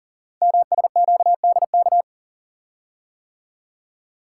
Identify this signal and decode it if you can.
Morse code, heard as a keyed tone, about 38 wpm, 700 Hz. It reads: MSQDK
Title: Code Groups - 5 Letters - 40wpm